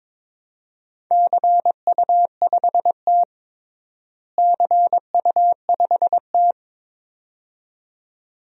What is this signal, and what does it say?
Morse code, heard as a keyed tone, about 22 wpm, 700 Hz. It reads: CU5T CU5T
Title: DXCC-100 Call Signs - 4 Characters - Repeated 2x - 22wpm